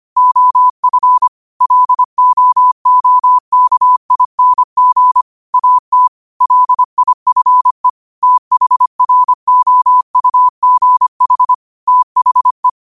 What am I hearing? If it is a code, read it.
Morse code, heard as a keyed tone, about 25 wpm, 1000 Hz. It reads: OF LOOKING AT LIFE THROUGH THE
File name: qotdfiles / 25wpm